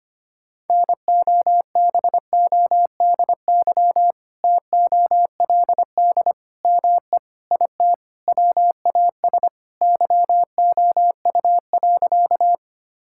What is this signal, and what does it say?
Morse code, heard as a keyed tone, about 25 wpm, 700 Hz. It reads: NOBODY TOLD ME IT WAS YOU.
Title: Sentences from Top 1600 Words 25wpm